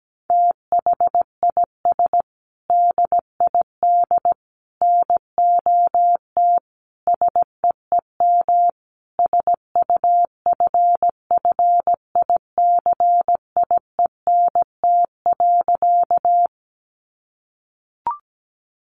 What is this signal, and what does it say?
Morse code, heard as a keyed tone, about 17 wpm, 700 Hz. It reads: THIS DID NOT SEEM SUFFICIENT.
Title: Sentences from Top 1400 Words 17wpm